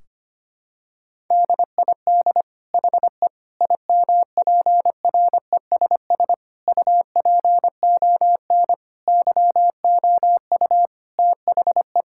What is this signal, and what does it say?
Morse code, heard as a keyed tone, about 25 wpm, 700 Hz. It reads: DID HE IMPRESS UPON YOU THE
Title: Sentences from Top 1600 Words 25wpm